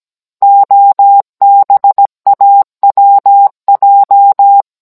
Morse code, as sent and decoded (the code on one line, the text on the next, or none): --- -... .- .-- .---
OBAWJ